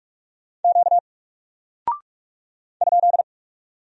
Morse code, heard as a keyed tone, about 44 wpm, 700 Hz. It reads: Q ?
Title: Single Character (Letter:Number:Punctuation:Prosign) Alt 45wpm